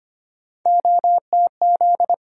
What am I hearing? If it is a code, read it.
Morse code, heard as a keyed tone, about 25 wpm, 700 Hz. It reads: OTZ
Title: USA Call Sign Postfixes - 3 Letters 25wpm